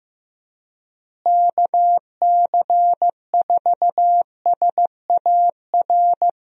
Morse code, heard as a keyed tone, about 16 wpm, 700 Hz. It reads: KC4SAR
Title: USA Call Signs - 6 Characters - Repeated 2x - 15wpm